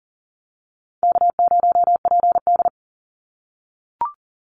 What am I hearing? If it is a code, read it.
Morse code, heard as a keyed tone, about 38 wpm, 700 Hz. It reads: K0PD
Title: Call Signs - 40wpm